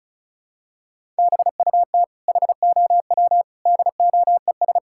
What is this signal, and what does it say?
Morse code, heard as a keyed tone, about 34 wpm, 700 Hz. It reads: BUT HOW DOES
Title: Sentences from Top 1000 Words - Review Part 6 - 35wpm